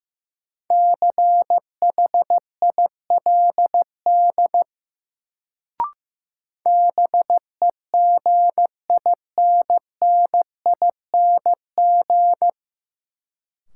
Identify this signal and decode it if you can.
Morse code, heard as a keyed tone, about 15 wpm, 700 Hz. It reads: CHILD BEGINNING
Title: Sentences from Top 800 Words - New Word Drill - 15wpm